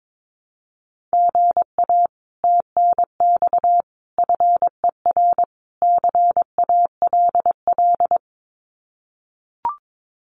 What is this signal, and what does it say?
Morse code, heard as a keyed tone, about 22 wpm, 700 Hz. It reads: GA TNX FER CALL
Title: Common QSO Phrases Encore 22wpm